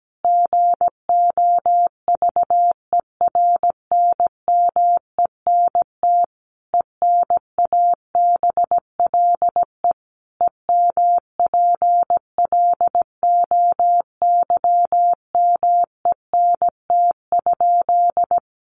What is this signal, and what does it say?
Morse code, heard as a keyed tone, about 17 wpm, 700 Hz. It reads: GOVERNMENT ENABLE EMPLOYMENT?
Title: Sentences from Top 1500 Words 17wpm